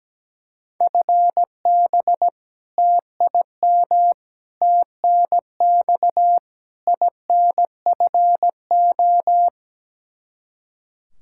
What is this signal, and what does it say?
Morse code, heard as a keyed tone, about 17 wpm, 700 Hz. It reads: FB TIM TNX INFO
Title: Practice QSO - Exchange 3 - More Info - 17wpm